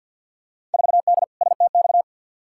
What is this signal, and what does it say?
Morse code, heard as a keyed tone, about 48 wpm, 700 Hz. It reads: 4D STX
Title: Field Day Exchanges Encore - 50wpm